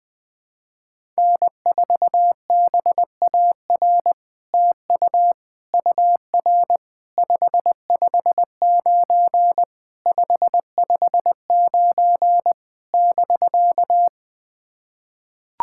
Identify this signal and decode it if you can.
Morse code, heard as a keyed tone, about 20 wpm, 700 Hz. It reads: N4BAR TU UR 559 559 <BK>
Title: SOTA QSO Practice 20wpm